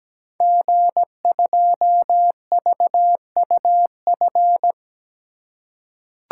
Morse code, heard as a keyed tone, about 17 wpm, 700 Hz. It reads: G2VUF
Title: Code Groups - 5 Letter:Numbers - 17wpm